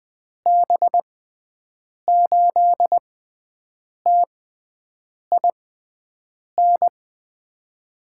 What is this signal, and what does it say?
Morse code, heard as a keyed tone, about 20 wpm, spaced as for 7 wpm, 700 Hz. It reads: B8TIN
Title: USA Call Signs - 6 Characters - 6x Character Spacing - 20wpm